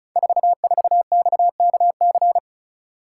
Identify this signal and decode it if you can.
Morse code, heard as a keyed tone, about 36 wpm, 700 Hz. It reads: 44XKC
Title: Code Groups - 5 Letter:Numbers - 35wpm